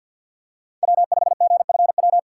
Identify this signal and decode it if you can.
Morse code, heard as a keyed tone, about 52 wpm, 700 Hz. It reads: W5GFW
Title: USA Call Signs - 2x Character Spacing - 50wpm